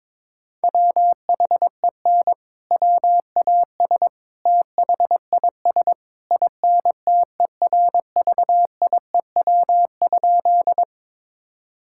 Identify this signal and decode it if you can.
Morse code, heard as a keyed tone, about 22 wpm, 700 Hz. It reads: WHEN WAS THIS INTERVIEW?
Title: Sentences from Top 1600 Words 22wpm